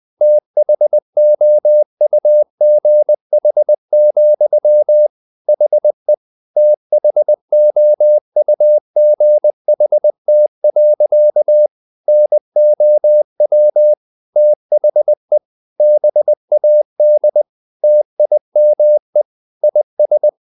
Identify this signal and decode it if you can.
Morse code, heard as a keyed tone, about 20 wpm, 600 Hz. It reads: THOUGH, HE THOUGHT. NOW THE BAD TIME IS
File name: the old man and the sea part 2 20wpm